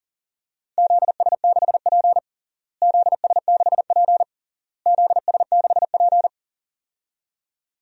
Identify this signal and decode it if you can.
Morse code, heard as a keyed tone, about 40 wpm, 700 Hz. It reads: ZS6P ZS6P ZS6P
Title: DXCC-100 Call Signs - 4 Characters - Repeated 3x - 40wpm